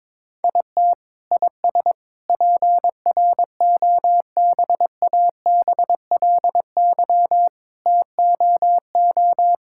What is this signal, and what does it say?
Morse code, heard as a keyed tone, about 21 wpm, 700 Hz. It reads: IT IS PROBABLY TOO
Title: Sentences from Top 1000 Words - Review Part 6 - 22wpm